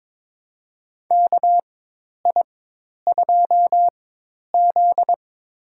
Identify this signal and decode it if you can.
Morse code, heard as a keyed tone, about 21 wpm, spaced as for 10 wpm, 700 Hz. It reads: KI2Z
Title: USA Call Signs - 4 Characters - 4x Character Spacing - 22wpm